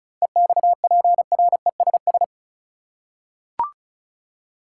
Morse code, heard as a keyed tone, about 34 wpm, 700 Hz. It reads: EXPRESS
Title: Seven Letter Words - 35wpm